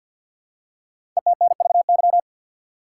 Morse code, heard as a keyed tone, about 50 wpm, 700 Hz. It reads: ETNVY
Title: Code Groups - 5 Letters - 50wpm